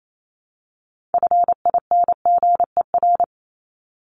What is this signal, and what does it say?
Morse code, heard as a keyed tone, about 28 wpm, 700 Hz. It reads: FINGER
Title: Top 1501-1750 Words 28wpm